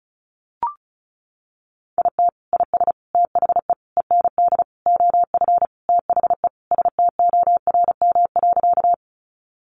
Morse code, heard as a keyed tone, about 34 wpm, 700 Hz. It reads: IT IS THE END OF THE STORM.
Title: Sentences from Top 1500 Words 35wpm